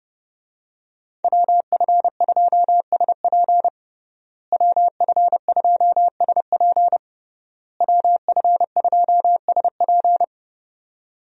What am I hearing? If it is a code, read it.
Morse code, heard as a keyed tone, about 29 wpm, 700 Hz. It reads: WF2SP WF2SP WF2SP
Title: USA Call Signs - 5 Characters - Repeated 3x - 30wpm